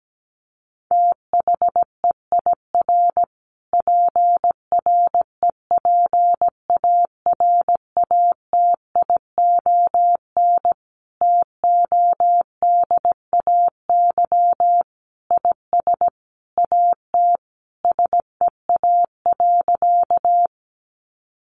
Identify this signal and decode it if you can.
Morse code, heard as a keyed tone, about 17 wpm, 700 Hz. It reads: THEIR PREPARATION TODAY IS AT SEA.
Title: Sentences from Top 1400 Words 17wpm